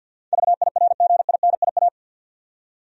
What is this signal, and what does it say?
Morse code, heard as a keyed tone, about 46 wpm, 700 Hz. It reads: VIRGINIA
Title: USA Common Names - 50wpm